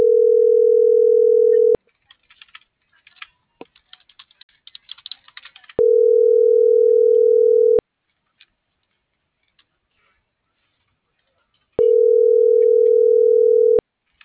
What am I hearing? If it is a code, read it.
Ringback tone (North America), 440 + 480 Hz.